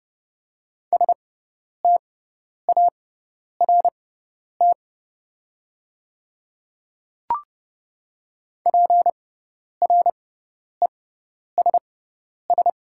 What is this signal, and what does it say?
Morse code, heard as a keyed tone, about 30 wpm, spaced as for 10 wpm, 700 Hz. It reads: START PRESS
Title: 6x Character Spacing - Top 300 Words - 30wpm